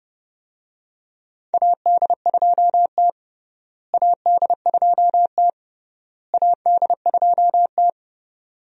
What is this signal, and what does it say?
Morse code, heard as a keyed tone, about 30 wpm, 700 Hz. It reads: AD2T AD2T AD2T
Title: USA Call Signs - 4 Characters - Repeated 3x - 30wpm